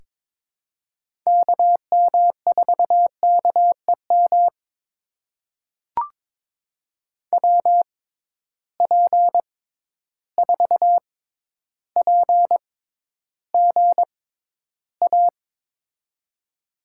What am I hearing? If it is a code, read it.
Morse code, heard as a keyed tone, about 22 wpm, spaced as for 11 wpm, 700 Hz. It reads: KM4KEM WP4PGA
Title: USA Call Signs - 6 Characters - 6x Character Spacing - 22wpm